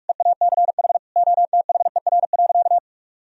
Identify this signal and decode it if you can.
Morse code, heard as a keyed tone, about 43 wpm, 700 Hz. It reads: EACH OTHER.
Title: Sentences from Top 1000 Words - Review Part 1 - 45wpm